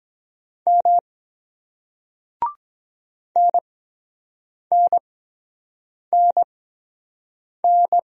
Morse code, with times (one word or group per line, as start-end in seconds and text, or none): 0.7-1.0 M
3.4-8.0 NNNN